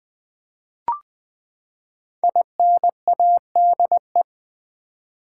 Morse code, heard as a keyed tone, about 20 wpm, 700 Hz. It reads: INADE
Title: Build A Word - 5 Letters At A Time - 10-Letter Words - 20wpm